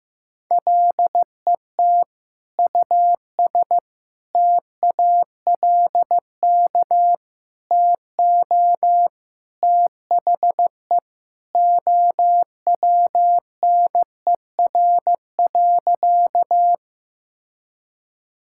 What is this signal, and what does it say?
Morse code, heard as a keyed tone, about 16 wpm, 700 Hz. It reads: LET US TALK TO THE OWNER.
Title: Sentences from Top 1600 Words 15wpm